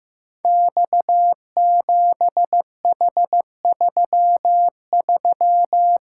Morse code, heard as a keyed tone, about 15 wpm, 700 Hz. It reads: X7H33
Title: Code Groups - 5 Letter:Numbers - 15wpm